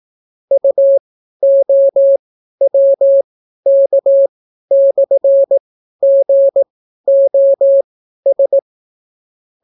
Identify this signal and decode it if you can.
Morse code, heard as a keyed tone, about 18 wpm, spaced as for 12 wpm, 550 Hz. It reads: UOWK/GOS